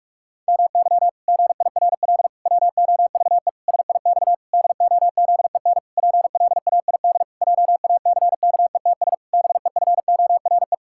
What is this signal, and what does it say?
Morse code, heard as a keyed tone, about 46 wpm, 700 Hz. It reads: MY GIRL WOVE SIX DOZEN PLAID JACKETS BEFORE
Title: Pangram Challenge v2 - 45wpm